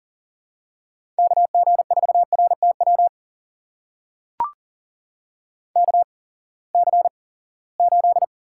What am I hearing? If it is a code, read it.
Morse code, heard as a keyed tone, about 37 wpm, spaced as for 22 wpm, 700 Hz. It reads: KG4RTW KC8
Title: USA Call Signs - 6 Characters - 8x Character Spacing - 40wpm